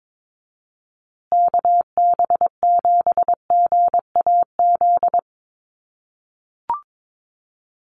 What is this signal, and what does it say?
Morse code, heard as a keyed tone, about 22 wpm, 700 Hz. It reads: KB7GAZ
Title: USA Call Signs Repeated 2x 22wpm